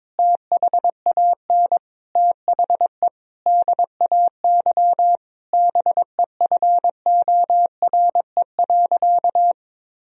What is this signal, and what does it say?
Morse code, heard as a keyed tone, about 22 wpm, 700 Hz. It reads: THAN THE DAY BEFORE.
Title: Sentences from Top 1000 Words - Review Part 1 - 22wpm